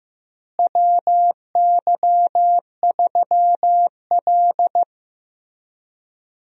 Morse code, heard as a keyed tone, about 16 wpm, 700 Hz. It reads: WY3L
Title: USA Call Signs - 4 Characters - Repeated 3x - 15wpm